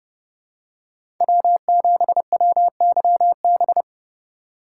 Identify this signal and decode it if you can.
Morse code, heard as a keyed tone, about 32 wpm, 700 Hz. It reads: W7WYB
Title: USA Call Signs - 5 Characters - 3x Character Spacing - 30wpm